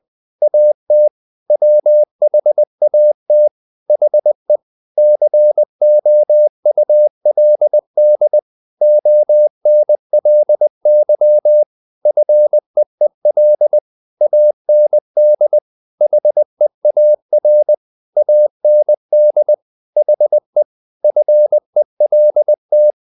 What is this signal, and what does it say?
Morse code, heard as a keyed tone, about 20 wpm, 600 Hz. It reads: AT WHAT HE COULD ONLY FEEL AND HEAR AND HE FELT
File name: the old man and the sea part 2 20wpm